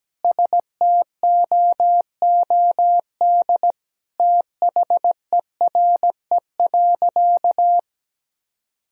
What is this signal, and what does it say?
Morse code, heard as a keyed tone, about 17 wpm, 700 Hz. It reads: STOOD THERE.
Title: Sentences from Top 1000 Words - Review Part 2 - 17wpm